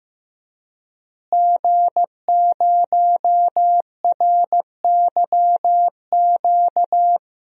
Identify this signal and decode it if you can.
Morse code, heard as a keyed tone, about 15 wpm, 700 Hz. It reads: G0RYQ
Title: DXCC-100 Call Signs - 5 Characters - Repeated 3x - 15wpm